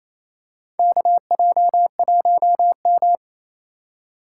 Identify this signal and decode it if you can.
Morse code, heard as a keyed tone, about 28 wpm, 700 Hz. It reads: KJ1M
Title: USA Call Signs - 4 Characters - 3x Character Spacing - 28wpm